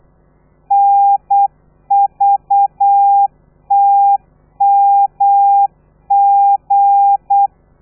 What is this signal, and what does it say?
Morse code, heard as a keyed tone, about 8 wpm, 800 Hz. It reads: NVTMG
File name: telegrafia / i2tfj / Lez13 35